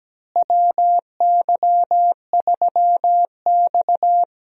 Morse code, heard as a keyed tone, about 17 wpm, 700 Hz. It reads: WY3X
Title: USA Call Signs - 4 Characters - Repeated 2x - 17wpm